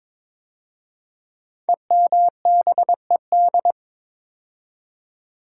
Morse code, heard as a keyed tone, about 22 wpm, 700 Hz. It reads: EMBED
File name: 5 Letter Words - First 3 Letters Given - 22wpm